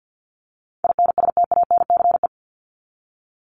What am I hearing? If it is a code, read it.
Morse code, heard as a keyed tone, about 50 wpm, 700 Hz. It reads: INSTANCE
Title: Top 501-750 Words 50wpm